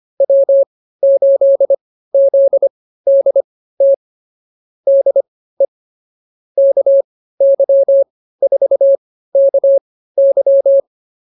Morse code, heard as a keyed tone, about 24 wpm, spaced as for 15 wpm, 550 Hz. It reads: W8ZDT DE KY4KY